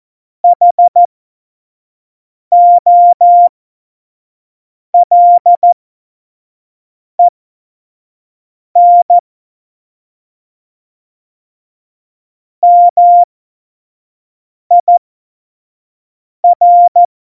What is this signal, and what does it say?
Morse code, heard as a keyed tone, about 14 wpm, spaced as for 5 wpm, 700 Hz. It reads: HOLEN MIR
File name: Morsekurs / Witze / Witz 12 Mai 2020, 25 F70